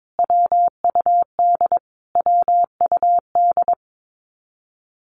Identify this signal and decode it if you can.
Morse code, heard as a keyed tone, about 22 wpm, 700 Hz. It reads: WUD WUD
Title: Abbreviations Repeated 3x - 22wpm